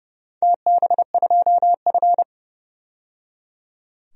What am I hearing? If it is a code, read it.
Morse code, heard as a keyed tone, about 30 wpm, 700 Hz. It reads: TB2F
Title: DXCC-100 Call Signs - 4 Characters - 30wpm